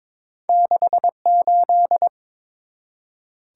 Morse code, heard as a keyed tone, about 22 wpm, 700 Hz. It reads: 68
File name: practice / 22wpm / Two Letter-Number Combinations - 22wpm